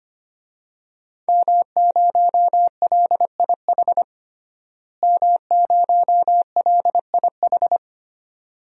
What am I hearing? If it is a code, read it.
Morse code, heard as a keyed tone, about 25 wpm, 700 Hz. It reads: M0LIH M0LIH
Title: DXCC-100 Call Signs - 5 Characters - Repeated 2x - 25wpm